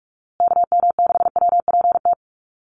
Morse code, heard as a keyed tone, about 44 wpm, 700 Hz. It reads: KM6WPT
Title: USA Call Signs Repeated 3x 45wpm